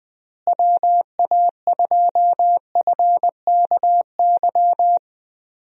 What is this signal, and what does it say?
Morse code, heard as a keyed tone, about 20 wpm, 700 Hz. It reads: WA2FKY
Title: USA Call Signs - 6 Characters - Repeated 3x - 20wpm